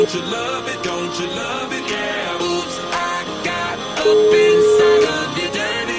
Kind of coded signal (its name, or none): telephone call progress tone